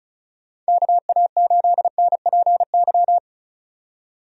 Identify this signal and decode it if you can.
Morse code, heard as a keyed tone, about 34 wpm, 700 Hz. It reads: KA8NPY
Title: USA Call Signs - 6 Characters - 35wpm